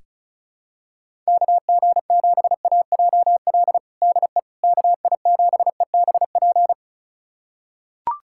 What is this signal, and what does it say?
Morse code, heard as a keyed tone, about 36 wpm, 700 Hz. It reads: KG7AJL DE KI7EBP
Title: USA Callsign DE Callsign - 6 Characters - 35wpm